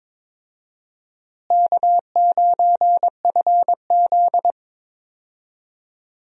Morse code, heard as a keyed tone, about 22 wpm, 700 Hz. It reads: K9FZ
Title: USA Call Signs - 4 Characters - Repeated 2x - 22wpm